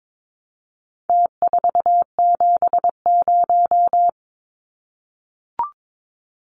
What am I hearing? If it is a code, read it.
Morse code, heard as a keyed tone, about 22 wpm, 700 Hz. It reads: T470
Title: Numbers - Contest Style - 4-Digit 22wpm